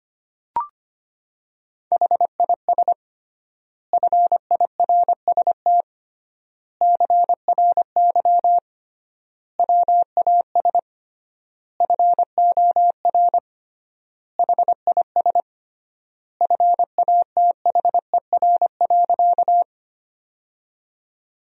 Morse code, heard as a keyed tone, about 25 wpm, 700 Hz. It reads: HIS FIRST CRY WAS FOR HIS FATHER.
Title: 3x Word-Spacing - Sentences from Top 500 Words - 25wpm